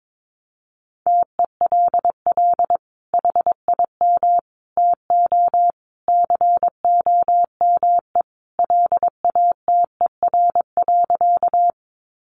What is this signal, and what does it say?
Morse code, heard as a keyed tone, about 22 wpm, 700 Hz. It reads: TELL HIM TO COME LATER.
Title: Sentences from Top 700 Words 22wpm